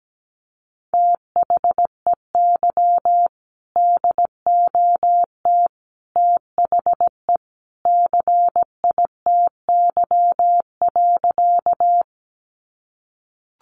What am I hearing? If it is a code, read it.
Morse code, heard as a keyed tone, about 17 wpm, 700 Hz. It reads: THEY DOT THE CITY.
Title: Sentences from Top 1500 Words 17wpm